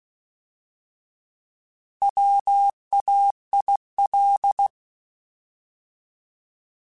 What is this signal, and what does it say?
Morse code, heard as a keyed tone, about 16 wpm, 800 Hz. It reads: WAIL
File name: Alphabetical / Alphabetical mp3s / Ws @15